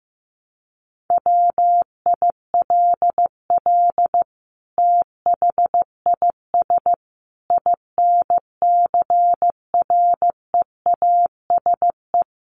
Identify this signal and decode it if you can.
Morse code, heard as a keyed tone, about 15 wpm, 700 Hz. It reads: WILL THIS INCREASE
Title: Sentences from Top 1200 Words 15wpm